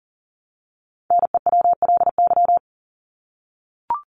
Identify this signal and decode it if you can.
Morse code, heard as a keyed tone, about 40 wpm, 700 Hz. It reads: NEWLY
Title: LY Suffix - Top 100 Words with Base Words 40wpm